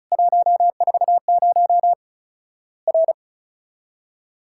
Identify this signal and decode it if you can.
Morse code, heard as a keyed tone, about 34 wpm, 700 Hz. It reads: 140 R
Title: Worked All Europe DX Contest - QTC Exchange - 35wpm